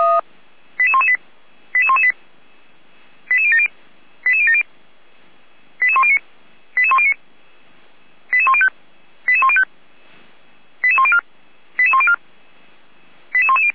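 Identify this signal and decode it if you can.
Five-tone selective call (ZVEI), heard as quick runs of five tones, tones of about 70 ms. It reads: 88118 88118 88070 88070 88109 88109 88105 88105 88104 88104 88110
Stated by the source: ZVEI five-tone call 88192 received by SDR, feuerwehr-orlamuende.org fire brigade audio server.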